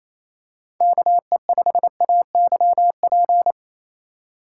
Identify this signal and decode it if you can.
Morse code, heard as a keyed tone, about 28 wpm, 700 Hz. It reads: KE5AYP
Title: USA Call Signs - 4x Character Spacing - 28wpm